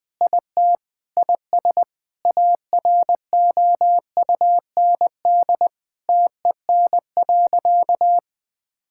Morse code, heard as a keyed tone, about 20 wpm, 700 Hz. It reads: IT IS AROUND TEN.
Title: Sentences with 6-Word Phrases - 20wpm